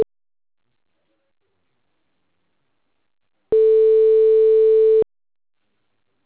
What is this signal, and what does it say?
Ringback tone (France), 440 Hz.